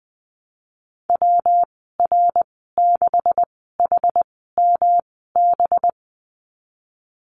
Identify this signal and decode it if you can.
Morse code, heard as a keyed tone, about 20 wpm, spaced as for 15 wpm, 700 Hz. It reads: WR6HMB
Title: 213 Lesson 34g - Number 8 - Call Signs 20wpm